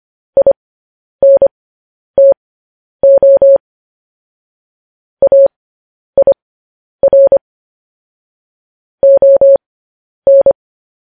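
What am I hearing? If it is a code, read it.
Morse code, heard as a keyed tone, about 25 wpm, spaced as for 10 wpm, 550 Hz. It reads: INTO AIR ON
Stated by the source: Morse code practice of news headlines at 10 wpm.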